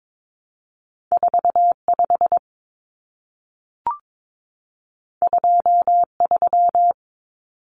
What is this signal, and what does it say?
Morse code, heard as a keyed tone, about 22 wpm, 700 Hz. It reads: 45 23
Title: Numbers - 2-Digit 22wpm